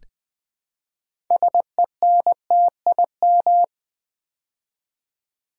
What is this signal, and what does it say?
Morse code, heard as a keyed tone, about 20 wpm, 700 Hz. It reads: SENTIM
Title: Build A Word - 3 Letters At A Time - 9-Letter Words - 20wpm